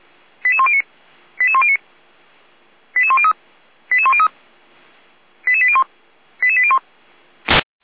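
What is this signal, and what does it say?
Five-tone selective call (ZVEI), heard as quick runs of five tones, tones of about 70 ms. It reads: 88109 88109 88103 88103 88801 88801
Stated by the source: ZVEI five-tone selective call received via SDR.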